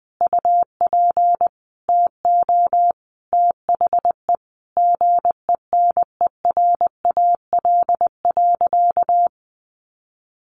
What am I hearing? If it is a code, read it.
Morse code, heard as a keyed tone, about 20 wpm, 700 Hz. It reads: UP TO THE GENERAL.
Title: Sentences from Top 1500 Words 20wpm